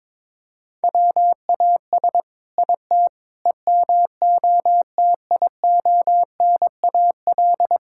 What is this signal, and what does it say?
Morse code, heard as a keyed tone, about 21 wpm, 700 Hz. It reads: WAS IT EMOTIONAL
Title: Sentences from Top 1000 Words - Review Part 5 - 22wpm